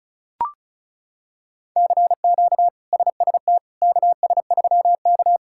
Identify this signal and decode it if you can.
Morse code, heard as a keyed tone, about 36 wpm, 700 Hz. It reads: CQ SST KS3K
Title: K1USN SST QSO Practice - 35wpm